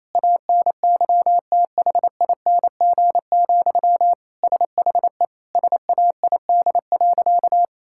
Morse code, heard as a keyed tone, about 28 wpm, 700 Hz. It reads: ANYTHING, SHE SAID.
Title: Sentences from Top 1000 Words - Review Part 6 - 28wpm